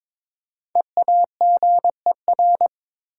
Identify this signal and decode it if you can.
Morse code, heard as a keyed tone, about 23 wpm, 700 Hz. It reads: EAGER
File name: 5 Letter Words - First 3 Letters Given - 22wpm